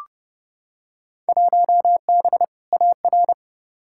Morse code, heard as a keyed tone, about 31 wpm, 700 Hz. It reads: 1B AR
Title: Field Day Exchanges Encore - 30wpm